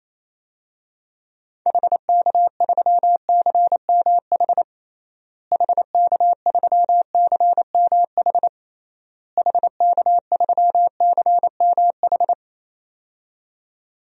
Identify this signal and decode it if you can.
Morse code, heard as a keyed tone, about 28 wpm, 700 Hz. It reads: HK3CMH HK3CMH HK3CMH
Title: DXCC-100 Call Signs - 6 Characters - Repeated 3x - 28wpm